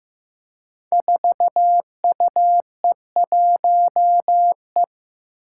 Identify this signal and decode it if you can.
Morse code, heard as a keyed tone, about 15 wpm, 700 Hz. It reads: 4UE1E
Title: Five Challenging Letter-Number Combinations 15wpm